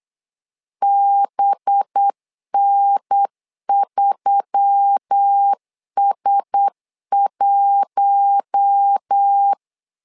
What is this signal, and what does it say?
Morse code, heard as a keyed tone, about 8 wpm, 800 Hz. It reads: BN3S1
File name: LEZIONI / 5 LBVC / 5 LB 8wpm 2816